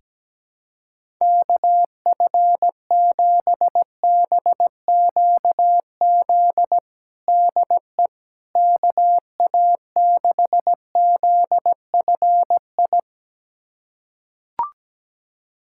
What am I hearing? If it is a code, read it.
Morse code, heard as a keyed tone, about 17 wpm, 700 Hz. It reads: KF7BQZ DE KA6ZFI
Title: USA Callsign DE Callsign - 6 Characters - 17wpm